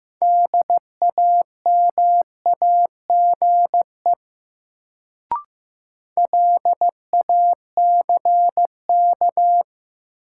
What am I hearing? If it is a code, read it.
Morse code, heard as a keyed tone, about 15 wpm, 700 Hz. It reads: DAMAGE LACK
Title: Sentences from Top 1100 Words - New Word Drill Alt - 15wpm